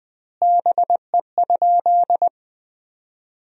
Morse code, heard as a keyed tone, about 20 wpm, 700 Hz. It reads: BE?
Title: 2x Word-Spacing - Sentences from Top 100 Words - 20wpm